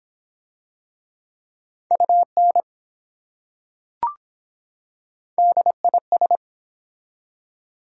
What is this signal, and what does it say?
Morse code, heard as a keyed tone, about 24 wpm, 700 Hz. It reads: UN DIS